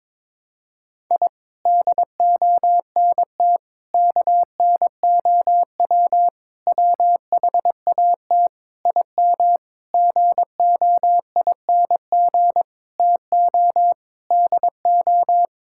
Morse code, heard as a keyed tone, about 22 wpm, 700 Hz. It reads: I DONT KNOW WHAT IM GOING TO DO
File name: practice / 22wpm / Sentences with 8-Word Phrases - 22wpm